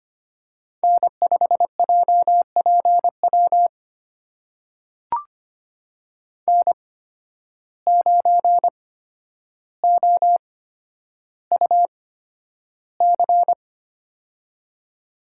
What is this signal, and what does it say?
Morse code, heard as a keyed tone, about 25 wpm, spaced as for 11 wpm, 700 Hz. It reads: N5JPW N9OUC
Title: USA Call Signs - 5 Characters - 8x Character Spacing - 25wpm